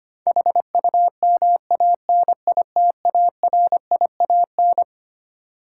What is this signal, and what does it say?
Morse code, heard as a keyed tone, about 25 wpm, 700 Hz. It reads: HUMANITARIAN
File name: Twelve Letter Words - 25wpm